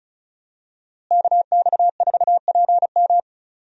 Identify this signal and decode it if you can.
Morse code, heard as a keyed tone, about 36 wpm, 700 Hz. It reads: KX4PM
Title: USA Call Signs - 5 Characters - Repeated 3x - 35wpm